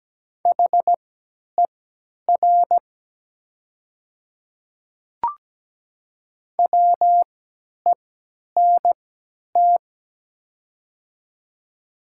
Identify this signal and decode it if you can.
Morse code, heard as a keyed tone, about 17 wpm, spaced as for 10 wpm, 700 Hz. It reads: HER WENT
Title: 3x Character Spacing - Top 300 Words - 17wpm